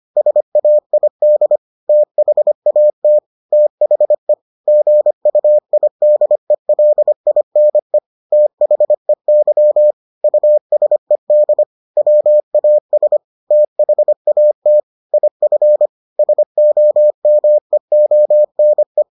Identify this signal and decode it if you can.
Morse code, heard as a keyed tone, about 25 wpm, 600 Hz. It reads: SAID THAT THE GUIDELINE THEY USED WAS THAT IF SOMEONE